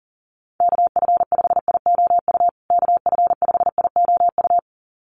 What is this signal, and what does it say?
Morse code, heard as a keyed tone, about 40 wpm, 700 Hz. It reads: KF5IOU KF5IOU
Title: USA Call Signs Repeated 2x 40wpm